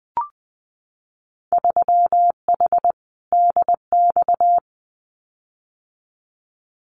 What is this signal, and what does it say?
Morse code, heard as a keyed tone, about 20 wpm, 700 Hz. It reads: 3H DX
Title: Winter Field Day Exchanges 20wpm